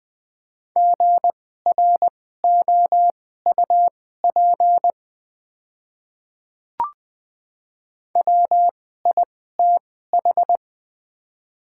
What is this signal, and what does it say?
Morse code, heard as a keyed tone, about 20 wpm, spaced as for 14 wpm, 700 Hz. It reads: GROUP WITH
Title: 2x Character Spacing - Top 300 Words - 20wpm